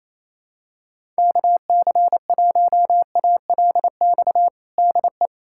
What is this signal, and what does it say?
Morse code, heard as a keyed tone, about 28 wpm, 700 Hz. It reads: KC1ALX DE
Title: USA Callsign DE Callsign - 6 Characters - 28wpm